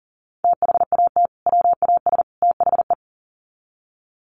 Morse code, heard as a keyed tone, about 40 wpm, 700 Hz. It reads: THAT WAS THE
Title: Sets of 3 Words 40wpm